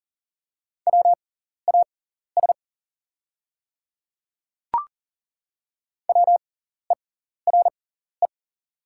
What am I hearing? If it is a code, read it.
Morse code, heard as a keyed tone, about 38 wpm, spaced as for 14 wpm, 690 Hz. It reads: WAS WERE
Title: 6x Character Spacing - Top 100 Words - 40wpm